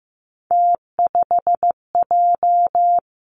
Morse code, heard as a keyed tone, about 15 wpm, 700 Hz. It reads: T5J
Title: Three Challenging Letter-Number Combinations 15wpm